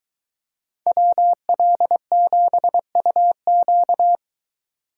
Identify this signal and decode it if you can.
Morse code, heard as a keyed tone, about 23 wpm, 700 Hz. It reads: WL7UQ